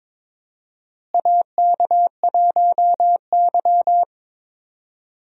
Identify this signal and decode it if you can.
Morse code, heard as a keyed tone, about 22 wpm, 700 Hz. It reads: AK1Y
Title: USA Call Signs - 4 Characters - Repeated 3x - 22wpm